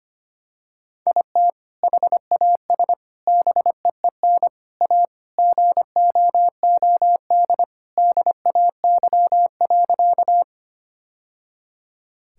Morse code, heard as a keyed tone, about 25 wpm, 700 Hz. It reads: IT HAS BEEN A GOOD DAY.
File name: practice / 25wpm / Sentences with 2-Word Phrases - 25wpm